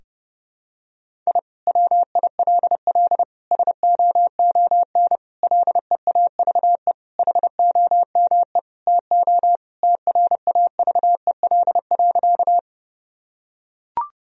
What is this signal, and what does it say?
Morse code, heard as a keyed tone, about 30 wpm, 700 Hz. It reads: I WILL SOON LEAVE HOME TO TRAVEL.
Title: Sentences from Top 1000 Words - Review Part 2 - 30wpm